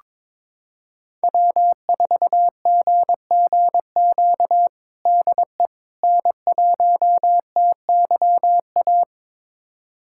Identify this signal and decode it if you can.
Morse code, heard as a keyed tone, about 22 wpm, 700 Hz. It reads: W4GGQ DE N1TYA
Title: USA Callsign DE Callsign - 5 Characters - 22wpm